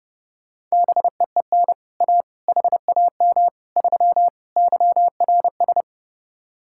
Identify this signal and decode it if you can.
Morse code, heard as a keyed tone, about 33 wpm, 700 Hz. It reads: BEEN A HAM 3 YRS